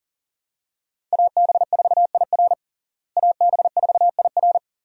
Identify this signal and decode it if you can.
Morse code, heard as a keyed tone, about 40 wpm, 700 Hz. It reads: AB4IR AB4IR
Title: USA Call Signs - 5 Characters - Repeated 2x - 40wpm